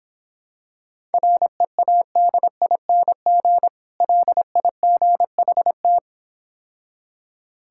Morse code, heard as a keyed tone, about 26 wpm, 700 Hz. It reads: READING LIGHT